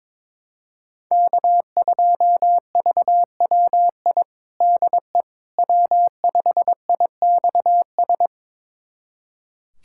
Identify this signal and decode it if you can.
Morse code, heard as a keyed tone, about 22 wpm, 700 Hz. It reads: K2VWI DE W5IXS
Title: USA Callsign DE Callsign - 5 Characters - 22wpm